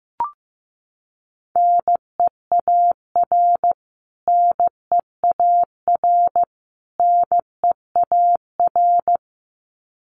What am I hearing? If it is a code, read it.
Morse code, heard as a keyed tone, about 15 wpm, 700 Hz. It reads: NEAR NEAR NEAR
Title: Top 500 Words Repeated 15wpm